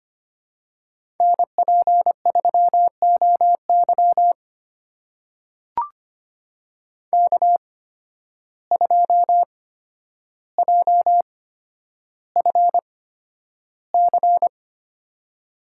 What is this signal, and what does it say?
Morse code, heard as a keyed tone, about 25 wpm, spaced as for 11 wpm, 700 Hz. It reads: NP3OY K2JFC
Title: USA Call Signs - 5 Characters - 8x Character Spacing - 25wpm